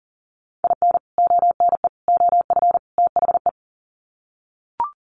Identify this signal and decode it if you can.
Morse code, heard as a keyed tone, about 40 wpm, 700 Hz. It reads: IN ONE OF THE
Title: Sets of 4 Words 40wpm